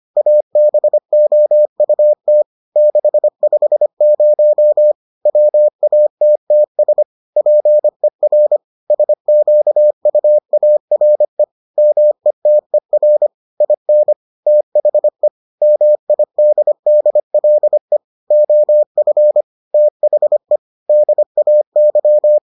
Morse code, heard as a keyed tone, about 25 wpm, 600 Hz. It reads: ABOUT 650 WATTS PER SQUARE METER IN THE MIDDLE OF THE DAY